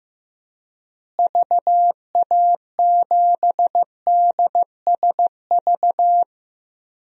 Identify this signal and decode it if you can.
Morse code, heard as a keyed tone, about 16 wpm, 700 Hz. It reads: VA7DSV
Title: DXCC-100 Call Signs - 6 Characters - Repeated 3x - 15wpm